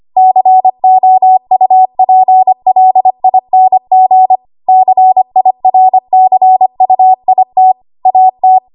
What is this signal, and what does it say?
Morse code, heard as a keyed tone, about 25 wpm, 750 Hz. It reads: COUPLING CIRCUIT AT
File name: Archive / 25 WPM / 25WPM